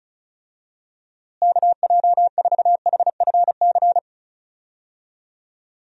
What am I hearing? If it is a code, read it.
Morse code, heard as a keyed tone, about 36 wpm, 700 Hz. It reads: KJ4HFC
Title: USA Call Signs - 6 Characters - Repeated 3x - 35wpm